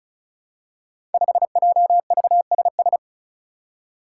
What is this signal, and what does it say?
Morse code, heard as a keyed tone, about 36 wpm, 700 Hz. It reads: 5JVSS
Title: Five Challenging Letter-Number Combinations 35wpm